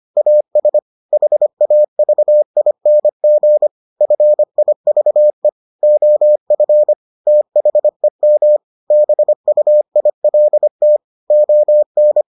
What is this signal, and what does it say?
Morse code, heard as a keyed tone, about 25 wpm, 600 Hz. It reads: AS HAVING FIVE OF THEM BUILT ON